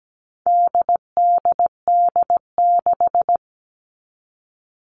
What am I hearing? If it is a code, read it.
Morse code, heard as a keyed tone, about 17 wpm, 700 Hz. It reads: DDD6
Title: All Cut Numbers - 4-Digit 17wpm